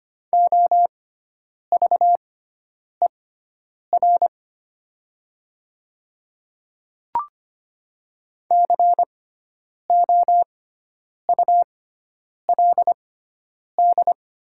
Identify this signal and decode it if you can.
Morse code, heard as a keyed tone, about 25 wpm, spaced as for 9 wpm, 700 Hz. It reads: OVER COULD
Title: 6x Character Spacing - Top 100 Words - 25wpm